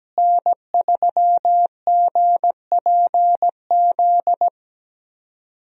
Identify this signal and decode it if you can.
Morse code, heard as a keyed tone, about 17 wpm, 700 Hz. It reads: N3GPZ
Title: USA Call Signs - 5 Characters - Repeated 2x - 17wpm